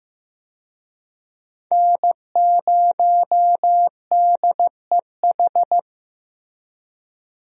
Morse code, heard as a keyed tone, about 15 wpm, 700 Hz. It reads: N0DEH
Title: USA Call Signs - 5 Characters - Repeated 3x - 15wpm